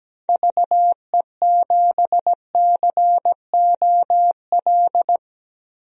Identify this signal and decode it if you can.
Morse code, heard as a keyed tone, about 17 wpm, 700 Hz. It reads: VE7COL
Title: DXCC-100 Call Signs - 6 Characters - 17wpm